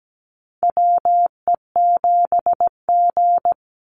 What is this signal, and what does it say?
Morse code, heard as a keyed tone, about 17 wpm, 700 Hz. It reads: WE7G
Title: Call Signs Repeated 3x - 17wpm